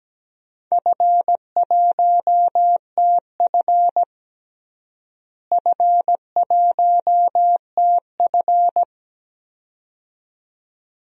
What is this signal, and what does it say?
Morse code, heard as a keyed tone, about 17 wpm, 700 Hz. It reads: F1TF F1TF
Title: DXCC-100 Call Signs - 4 Characters - Repeated 2x - 17wpm